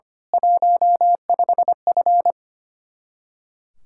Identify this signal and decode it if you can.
Morse code, heard as a keyed tone, about 25 wpm, 700 Hz. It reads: 15F
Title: Speed-Racing Common QSO Phrases Encore - 25wpm